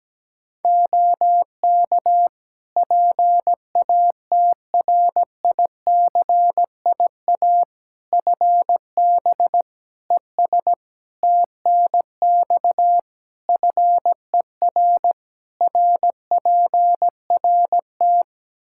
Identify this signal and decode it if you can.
Morse code, heard as a keyed tone, about 17 wpm, 700 Hz. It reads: OK PATRICIA FB ES TNX FER RPRT
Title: Practice QSO - Exchange 2 - Info - 17wpm